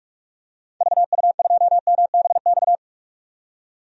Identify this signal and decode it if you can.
Morse code, heard as a keyed tone, about 43 wpm, 700 Hz. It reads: VU2MBX
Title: DXCC-100 Call Signs - 6 Characters - 45wpm